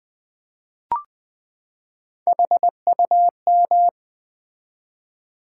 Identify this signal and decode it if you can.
Morse code, heard as a keyed tone, about 20 wpm, 700 Hz. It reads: HUM
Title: Build A Word - 3 Letters At A Time - 8-Letter Words - 20wpm